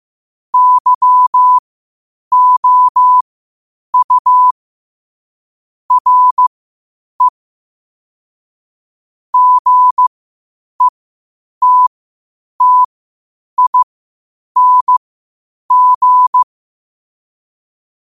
Morse code, heard as a keyed tone, about 15 wpm, spaced as for 8 wpm, 1000 Hz. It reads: YOU RE GETTING
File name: qotdfiles / 8wpm